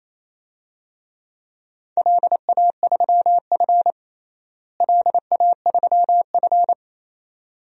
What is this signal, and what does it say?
Morse code, heard as a keyed tone, about 28 wpm, 700 Hz. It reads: LA3F LA3F
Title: DXCC-100 Call Signs - 4 Characters - Repeated 2x - 28wpm